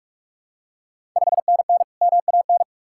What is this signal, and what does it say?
Morse code, heard as a keyed tone, about 46 wpm, 700 Hz. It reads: 5NN MAN
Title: Ontario QSO Party Exchange 45wpm